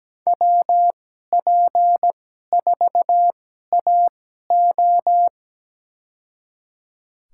Morse code, heard as a keyed tone, about 17 wpm, spaced as for 12 wpm, 700 Hz. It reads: WP4AO
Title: USA Call Signs - 5 Characters - 2x Character Spacing - 17wpm